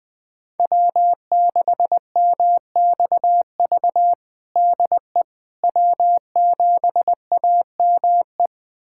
Morse code, heard as a keyed tone, about 20 wpm, 700 Hz. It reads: W6MXV DE W7AME
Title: USA Callsign DE Callsign - 5 Characters - 20wpm